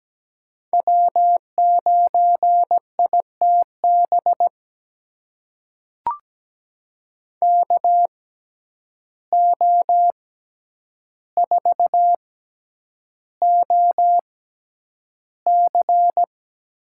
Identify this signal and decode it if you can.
Morse code, heard as a keyed tone, about 17 wpm, spaced as for 9 wpm, 700 Hz. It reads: W9ITB KO4OC
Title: USA Call Signs - 6x Character Spacing - 17wpm